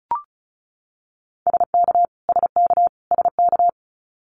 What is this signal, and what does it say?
Morse code, heard as a keyed tone, about 34 wpm, 700 Hz. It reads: SK SK SK
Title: Canadian Provinces and Territories - Repeated 3x - 35wpm